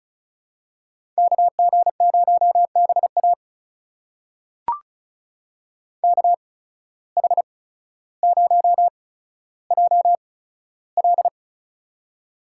Morse code, heard as a keyed tone, about 34 wpm, spaced as for 15 wpm, 700 Hz. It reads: KG0BA KH0JL
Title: USA Call Signs - 5 Characters - 8x Character Spacing - 35wpm